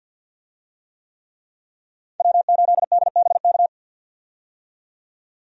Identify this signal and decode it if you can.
Morse code, heard as a keyed tone, about 52 wpm, 700 Hz. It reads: W8DBK
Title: USA Call Signs - 5 Characters - 50wpm